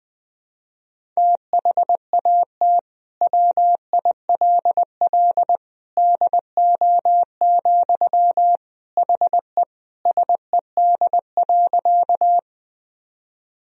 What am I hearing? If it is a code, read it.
Morse code, heard as a keyed tone, about 21 wpm, 700 Hz. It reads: THAT WILL DO, HE SED.
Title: Sentences from Top 100 Words and Abbreviations 20wpm